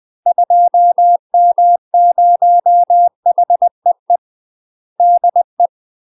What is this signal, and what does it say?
Morse code, heard as a keyed tone, about 20 wpm, 700 Hz. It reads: 2M0HEE DE
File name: UK MASTER QSOs TRAINING / 20WPM / UK MASTER QSOs 0018